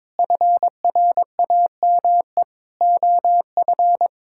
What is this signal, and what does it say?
Morse code, heard as a keyed tone, about 22 wpm, 700 Hz. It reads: FRAME OF
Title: Sentences from Top 1000 Words - Review Part 6 - 22wpm